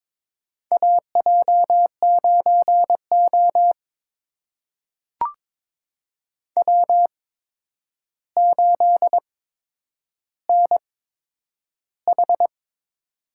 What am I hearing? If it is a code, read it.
Morse code, heard as a keyed tone, about 22 wpm, spaced as for 9 wpm, 700 Hz. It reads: AJ9O W8NH
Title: USA Call Signs - 4 Characters - 8x Character Spacing - 22wpm